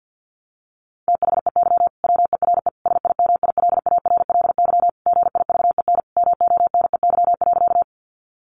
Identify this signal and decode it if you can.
Morse code, heard as a keyed tone, about 50 wpm, 700 Hz. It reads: THEY WERE SIMILARLY GIVEN MONEY.
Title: Sentences from Top 1400 Words 50wpm